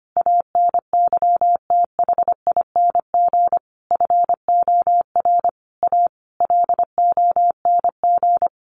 Morse code, heard as a keyed tone, about 25 wpm, 700 Hz. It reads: ANYTHING FOR A LONG
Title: Sentences from Top 1000 Words 25wpm